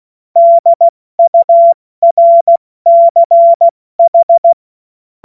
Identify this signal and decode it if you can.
Morse code, heard as a keyed tone, about 16 wpm, 680 Hz. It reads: DURCH
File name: Morsekurs / Witze / Witz 20. September 2022 70 F80